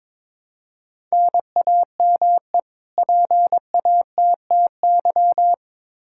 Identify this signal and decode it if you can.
Morse code, heard as a keyed tone, about 22 wpm, 700 Hz. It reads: NAME PATTY
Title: K1USN SST QSO Practice 22wpm